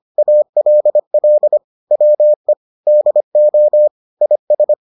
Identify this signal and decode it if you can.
Morse code, heard as a keyed tone, about 25 wpm, 600 Hz. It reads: ALL WE DO IS